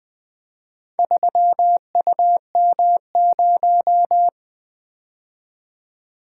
Morse code, heard as a keyed tone, about 20 wpm, 700 Hz. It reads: 3UM0
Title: Four Letter-Number Combinations - 20wpm